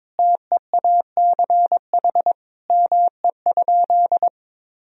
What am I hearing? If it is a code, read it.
Morse code, heard as a keyed tone, about 22 wpm, 700 Hz. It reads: TEACH ME?
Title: Sentences from Top 1000 Words - Review Part 2 - 22wpm